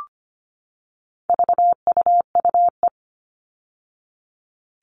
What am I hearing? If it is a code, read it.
Morse code, heard as a keyed tone, about 25 wpm, 700 Hz. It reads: VUUE
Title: All Cut Numbers - 4-Digit 25wpm